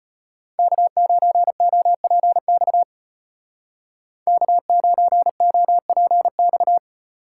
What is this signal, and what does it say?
Morse code, heard as a keyed tone, about 32 wpm, 700 Hz. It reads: K9OPX K9OPX